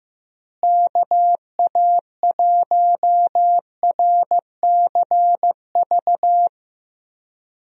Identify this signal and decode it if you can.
Morse code, heard as a keyed tone, about 15 wpm, 700 Hz. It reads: KA1RCV
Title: USA Call Signs - 6 Characters - Repeated 2x - 15wpm